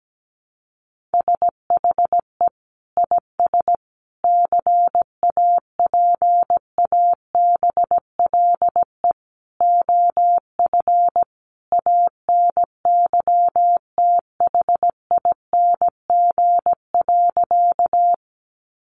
Morse code, heard as a keyed tone, about 17 wpm, 700 Hz. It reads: SHE IS CAPABLE OF ANYTHING.
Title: Sentences from Top 1200 Words 17wpm